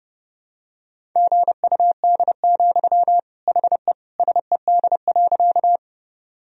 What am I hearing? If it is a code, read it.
Morse code, heard as a keyed tone, about 30 wpm, 700 Hz. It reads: GUD, HE SED.
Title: Sentences from Top 200 Words and Abbreviations 30wpm